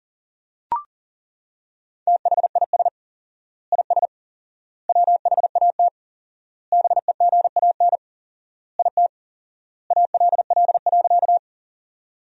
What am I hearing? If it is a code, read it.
Morse code, heard as a keyed tone, about 40 wpm, 700 Hz. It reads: THIS IS WHAT BEGAN IT ALL.
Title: 4x Word-Spacing - Sentences from Top 300 Words - 40wpm